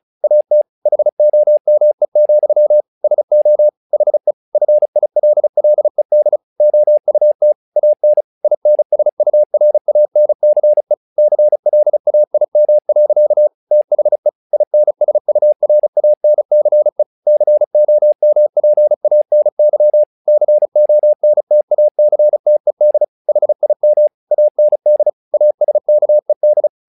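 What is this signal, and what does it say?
Morse code, heard as a keyed tone, about 36 wpm, 600 Hz. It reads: AT HOME, SO HE FILLED OUT AN INSURANCE CLAIM. THE INSURANCE COMPANY CONTACTED HIM AND ASKED
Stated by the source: Morse practice text "Tower Trouble" sent at 35 WPM, from the SKCC - Straight Key Century Club learning center audio files.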